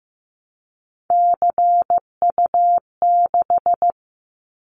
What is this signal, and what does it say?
Morse code, heard as a keyed tone, about 15 wpm, 700 Hz. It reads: CU6
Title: Callsign Prefixes for Top 100 DXCC - 15wpm